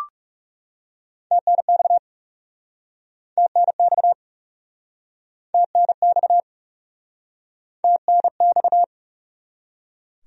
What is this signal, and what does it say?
Morse code, heard as a keyed tone, about 42 wpm, 700 Hz. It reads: TNX TNX TNX TNX